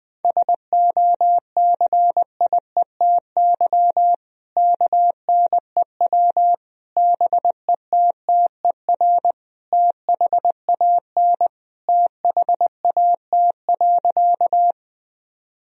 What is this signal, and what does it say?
Morse code, heard as a keyed tone, about 20 wpm, 700 Hz. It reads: SOCIETY KNEW BETTER THAN THAT.
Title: Sentences from Top 1000 Words - Review Part 4 - 20wpm